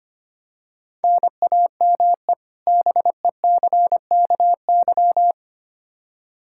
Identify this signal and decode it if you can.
Morse code, heard as a keyed tone, about 25 wpm, 700 Hz. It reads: NAME BECKY
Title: Practice QSO - Exchange 1 - Beginning - 25wpm